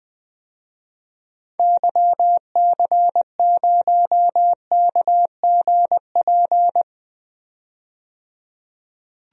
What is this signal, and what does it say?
Morse code, heard as a keyed tone, about 20 wpm, 700 Hz. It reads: YC0KGP
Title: DXCC-100 Call Signs - 6 Characters - 20wpm